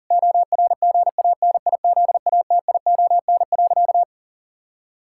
Morse code, heard as a keyed tone, about 40 wpm, 700 Hz. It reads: ORGANIZATION.
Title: Sentences from Top 1000 Words - Review Part 5 - 40wpm